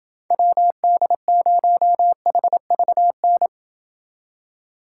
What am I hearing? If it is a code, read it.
Morse code, heard as a keyed tone, about 29 wpm, 700 Hz. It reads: WD0HVN